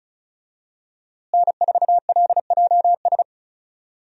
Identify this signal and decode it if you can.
Morse code, heard as a keyed tone, about 34 wpm, 700 Hz. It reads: N4LJS
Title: USA Call Signs - 5 Characters - Repeated 3x - 35wpm